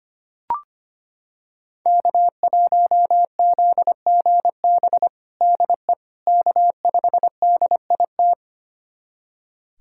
Morse code, heard as a keyed tone, about 25 wpm, 700 Hz. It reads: K1ZGB DE K5DIT
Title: USA Callsign DE Callsign - 5 Characters - 25wpm